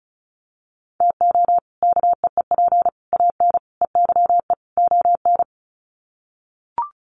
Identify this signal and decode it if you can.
Morse code, heard as a keyed tone, about 34 wpm, 700 Hz. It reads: TO KEEP AN EYE ON
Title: Sets of 5 Words Encore 35wpm